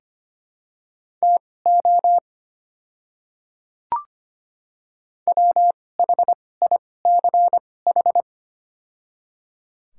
Morse code, heard as a keyed tone, about 25 wpm, spaced as for 18 wpm, 700 Hz. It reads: TO WHICH
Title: 2x Character Spacing - Top 100 Words - 25wpm